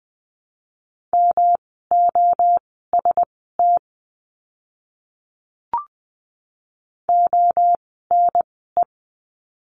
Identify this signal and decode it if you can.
Morse code, heard as a keyed tone, about 20 wpm, spaced as for 15 wpm, 700 Hz. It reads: MOST ONE
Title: 244 Lesson 39f - Letter z - Words 20wpm